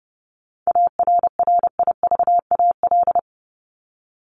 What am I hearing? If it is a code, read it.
Morse code, heard as a keyed tone, about 30 wpm, 700 Hz. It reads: ARRIVAL
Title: Top 1751-2000 Words 30wpm